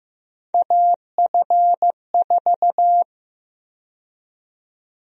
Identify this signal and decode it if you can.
Morse code, heard as a keyed tone, about 15 wpm, 700 Hz. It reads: AF4
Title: USA Call Sign Prefixes - 15wpm